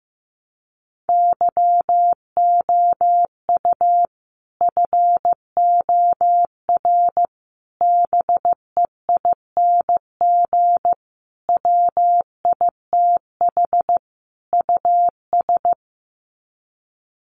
Morse code, heard as a keyed tone, about 15 wpm, 700 Hz. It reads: YOU FOR BEING WITH US
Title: Sets of 5 Words 15wpm